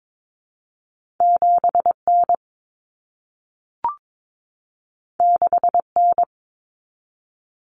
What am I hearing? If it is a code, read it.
Morse code, heard as a keyed tone, about 22 wpm, 700 Hz. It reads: 7N 6N
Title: Common Cut Numbers - 2-Digit 22wpm